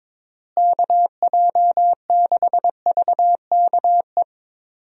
Morse code, heard as a keyed tone, about 22 wpm, 700 Hz. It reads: KJ6VKE
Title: USA Call Signs - 6 Characters - Repeated 3x - 22wpm